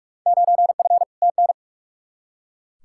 Morse code, heard as a keyed tone, about 42 wpm, 700 Hz. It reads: 9F TN
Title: Field Day Exchanges Encore - 45wpm